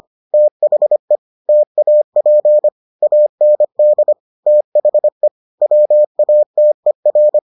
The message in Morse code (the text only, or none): THE TAP AND THE WATER